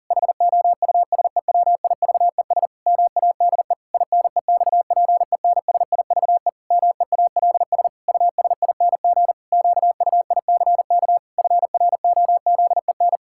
Morse code, heard as a keyed tone, about 40 wpm, 700 Hz. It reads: HOUSEWIVES MADE INEXPENSIVE MEALS USING QUICK FROZEN
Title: Pangram Challenge v2 - 40wpm